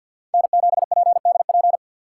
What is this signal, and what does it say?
Morse code, heard as a keyed tone, about 53 wpm, 700 Hz. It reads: N7PDP
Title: USA Call Signs - 5 Characters - 6x Character Spacing - 50wpm